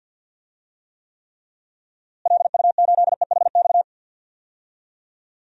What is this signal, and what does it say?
Morse code, heard as a keyed tone, about 52 wpm, 700 Hz. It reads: LU8EHX
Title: DXCC-100 Call Signs - 6 Characters - Repeated 3x - 50wpm